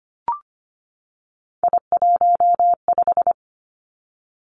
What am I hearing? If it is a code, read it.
Morse code, heard as a keyed tone, about 25 wpm, 700 Hz. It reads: I15
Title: Three Challenging Letter-Number Combinations 25wpm